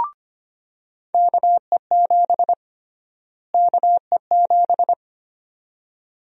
Morse code, heard as a keyed tone, about 25 wpm, 700 Hz. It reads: KE7 KE7
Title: USA Call Sign Prefixes - Repeated 2x - 25wpm